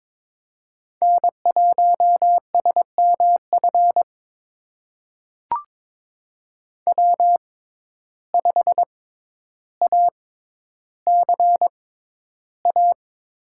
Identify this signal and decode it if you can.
Morse code, heard as a keyed tone, about 22 wpm, spaced as for 11 wpm, 700 Hz. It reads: N1SMF W5ACA
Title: USA Call Signs - 5 Characters - 6x Character Spacing - 22wpm